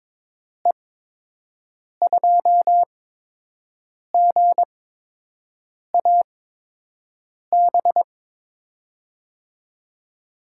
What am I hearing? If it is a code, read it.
Morse code, heard as a keyed tone, about 22 wpm, spaced as for 6 wpm, 700 Hz. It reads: E2GAB
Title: USA Call Signs - 6 Characters - 8x Character Spacing - 22wpm